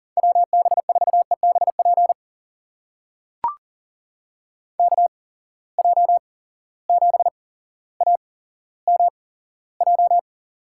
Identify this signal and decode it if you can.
Morse code, heard as a keyed tone, about 40 wpm, spaced as for 17 wpm, 700 Hz. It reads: WB4EBP KJ7AMJ
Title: USA Call Signs - 6 Characters - 8x Character Spacing - 40wpm